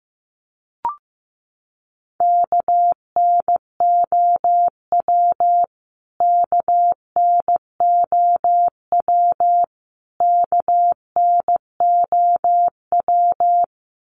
Morse code, heard as a keyed tone, about 15 wpm, 700 Hz. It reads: KNOW KNOW KNOW
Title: Top 500 Words Repeated 15wpm